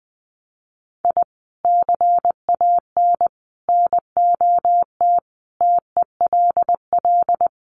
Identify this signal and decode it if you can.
Morse code, heard as a keyed tone, about 20 wpm, 700 Hz. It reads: I CAN NOT TELL
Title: Sets of 5 Words 20wpm